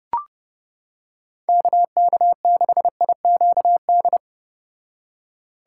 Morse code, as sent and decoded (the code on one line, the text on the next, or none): -.- -.- -.... .. --.- -..
KK6IQD